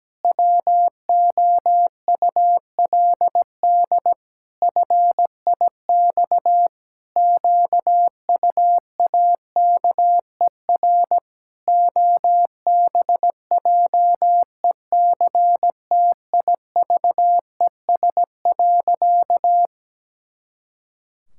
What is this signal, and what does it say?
Morse code, heard as a keyed tone, about 17 wpm, 700 Hz. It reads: WOULD FIX QUAKER OBJECTIVES.
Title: Pangram Challenge v2 - 17wpm